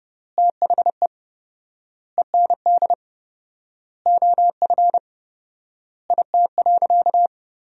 Morse code, heard as a keyed tone, about 30 wpm, 700 Hz. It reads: THE END OF IT.
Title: 4x Word-Spacing - Sentences from Top 200 Words - 30wpm